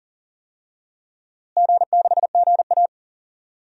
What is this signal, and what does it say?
Morse code, heard as a keyed tone, about 40 wpm, 700 Hz. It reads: G6GA
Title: DXCC-100 Call Signs - 4 Characters - Repeated 2x - 40wpm